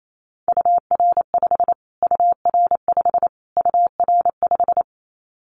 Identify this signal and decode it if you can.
Morse code, heard as a keyed tone, about 28 wpm, 700 Hz. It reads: UR5 UR5 UR5
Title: Callsign Prefixes for Top 25 DXCC - Repeated 3x - 28wpm